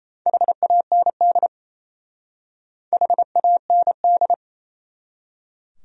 Morse code, heard as a keyed tone, about 34 wpm, 700 Hz. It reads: HAND HAND
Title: Speed-Racing Top 500 Words 28wpm